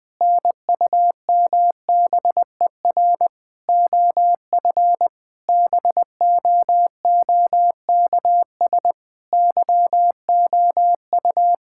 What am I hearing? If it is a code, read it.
Morse code, heard as a keyed tone, about 21 wpm, 700 Hz. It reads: NUMBER OF BOOKS YOU
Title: Sentences with 8-Word Phrases - Spoken Beginning - 20wpm